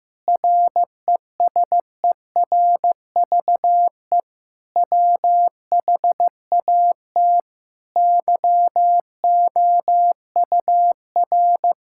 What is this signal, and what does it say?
Morse code, heard as a keyed tone, about 15 wpm, 700 Hz. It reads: RESERVE WHAT YOUR
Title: Sentences from Top 1000 Words - Review Part 6 - 15wpm